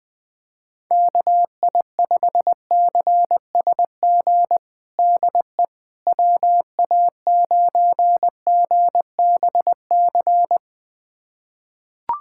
Morse code, heard as a keyed tone, about 21 wpm, 700 Hz. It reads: KI5CSG DE WA9GBC
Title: USA Callsign DE Callsign - 6 Characters - 20wpm